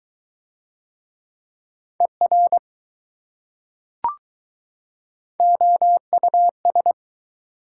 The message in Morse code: ER OUS

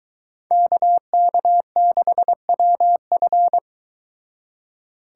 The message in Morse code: KK6WF

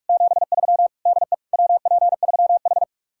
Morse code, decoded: 73 DE WP3H

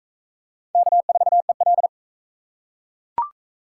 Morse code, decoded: K4EL

